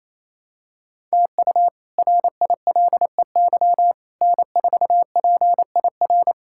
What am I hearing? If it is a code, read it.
Morse code, heard as a keyed tone, about 28 wpm, 700 Hz. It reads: TU RILEY N4PIR
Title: K1USN SST QSO Practice - 28wpm